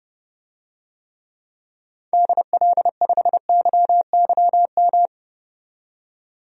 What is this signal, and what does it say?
Morse code, heard as a keyed tone, about 30 wpm, 700 Hz. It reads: DL5YYM
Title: DXCC-100 Call Signs - 6 Characters - 30wpm